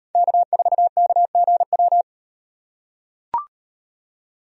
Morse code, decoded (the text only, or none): K4KGW